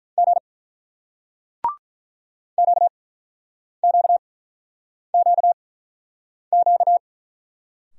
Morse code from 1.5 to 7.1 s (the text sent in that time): QQQQ